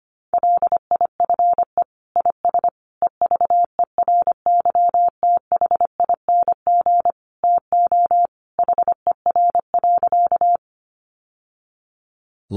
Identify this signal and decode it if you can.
Morse code, heard as a keyed tone, about 25 wpm, 700 Hz. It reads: LIFE IS EVERYTHING TO HER.
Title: Sentences from Top 700 Words 25wpm